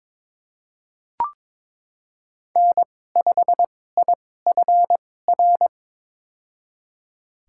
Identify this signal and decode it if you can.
Morse code, heard as a keyed tone, about 21 wpm, spaced as for 16 wpm, 700 Hz. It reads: N5IFR
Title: USA Call Signs - 2x Character Spacing - 22wpm